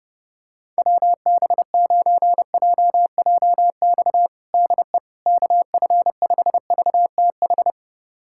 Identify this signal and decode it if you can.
Morse code, heard as a keyed tone, about 32 wpm, 700 Hz. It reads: WB9JJX DE KF5VTH
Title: USA Callsign DE Callsign - 6 Characters - 30wpm